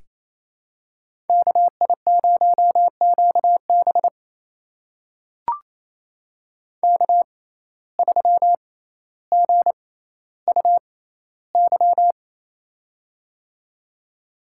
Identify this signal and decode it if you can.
Morse code, heard as a keyed tone, about 28 wpm, spaced as for 14 wpm, 700 Hz. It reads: KI0QB K3GUY
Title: USA Call Signs - 6x Character Spacing - 28wpm